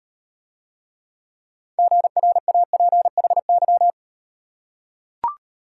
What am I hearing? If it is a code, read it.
Morse code, heard as a keyed tone, about 39 wpm, 700 Hz. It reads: GRAPHY